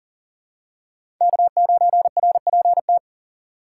Morse code, heard as a keyed tone, about 40 wpm, 700 Hz. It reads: K9RPT
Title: USA Call Signs - 5 Characters - 2x Character Spacing - 40wpm